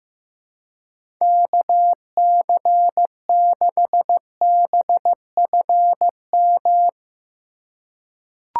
Morse code, heard as a keyed tone, about 15 wpm, 700 Hz. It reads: KC6BFM
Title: USA Call Signs - 6 Characters - 2x Character Spacing - 15wpm